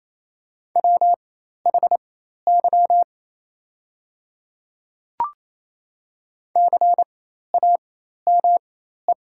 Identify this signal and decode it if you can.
Morse code, heard as a keyed tone, about 28 wpm, spaced as for 13 wpm, 700 Hz. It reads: WHY CAME
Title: 4x Character Spacing - Top 300 Words - 28wpm